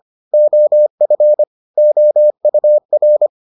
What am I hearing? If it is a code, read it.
Morse code, heard as a keyed tone, about 25 wpm, 600 Hz. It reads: OF OUR